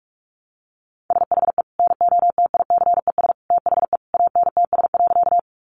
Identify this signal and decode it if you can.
Morse code, heard as a keyed tone, about 44 wpm, 700 Hz. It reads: SHE NOTICES THE ANTS.
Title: Additional Easy Phrases 45wpm